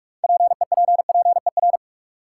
Morse code, heard as a keyed tone, about 45 wpm, 700 Hz. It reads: PEPPER
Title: Sets of 5 Words - New Word Drill - 45wpm